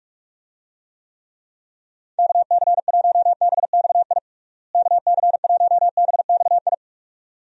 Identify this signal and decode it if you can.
Morse code, heard as a keyed tone, about 42 wpm, 700 Hz. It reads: KC1BXI KC1BXI
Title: USA Call Signs - 6 Characters - Repeated 2x - 45wpm